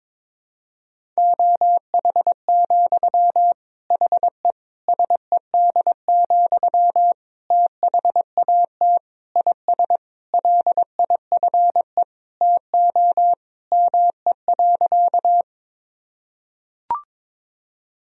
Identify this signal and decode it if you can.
Morse code, heard as a keyed tone, about 22 wpm, 700 Hz. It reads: OH, HE SED, THAT IS LIFE TO ME.
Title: Sentences from Top 500 Words and Abbreviations 22wpm